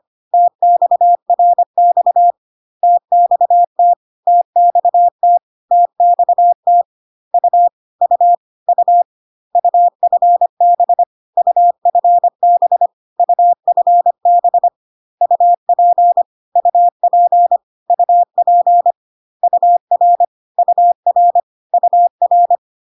Morse code, heard as a keyed tone, about 25 wpm, 700 Hz. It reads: TXRX TXT TXT TXT U U U UFB UFB UFB UP UP UP UR UR UR
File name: CW abbreviations alphabetical repeated 3 times 25 25wpm ik1hge